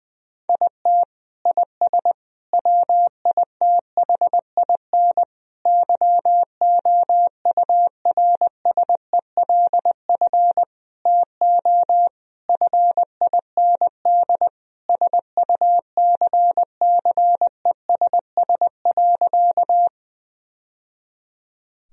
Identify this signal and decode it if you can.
Morse code, heard as a keyed tone, about 21 wpm, 700 Hz. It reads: IT IS WITHIN YOURSELF TO FIND SUCCESS.
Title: Sentences from Top 1000 Words - Review Part 4 - 20wpm